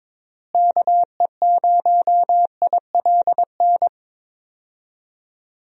Morse code, heard as a keyed tone, about 22 wpm, 700 Hz. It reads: KE0ILN
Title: USA Call Signs - 6 Characters - Repeated 2x - 22wpm